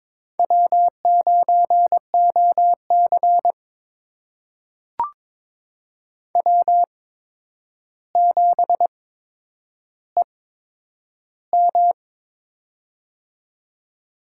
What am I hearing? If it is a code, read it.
Morse code, heard as a keyed tone, about 22 wpm, spaced as for 9 wpm, 700 Hz. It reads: W9OC W7EM
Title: USA Call Signs - 4 Characters - 8x Character Spacing - 22wpm